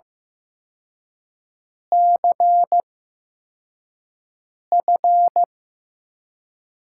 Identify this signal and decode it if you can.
Morse code, heard as a keyed tone, about 15 wpm, spaced as for 4 wpm, 700 Hz. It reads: CF
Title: USA Call Signs - 4 Characters - 8x Character Spacing - 15wpm